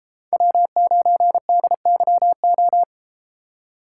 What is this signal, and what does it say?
Morse code, heard as a keyed tone, about 33 wpm, 700 Hz. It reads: W9DYO